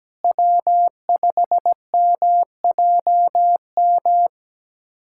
Morse code, heard as a keyed tone, about 17 wpm, 700 Hz. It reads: W5MJM
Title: USA Callsign DE Callsign - 5 Characters - 17wpm